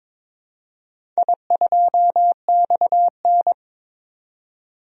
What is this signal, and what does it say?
Morse code, heard as a keyed tone, about 22 wpm, 700 Hz. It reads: I2XN
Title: DXCC-100 Call Signs - 4 Characters - 22wpm